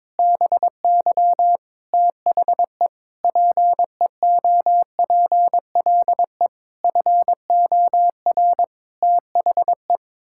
Morse code, heard as a keyed tone, about 22 wpm, 700 Hz. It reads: BY THE PEOPLE FOR THE
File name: practice / 22wpm / Sets of 8 Words - 22wpm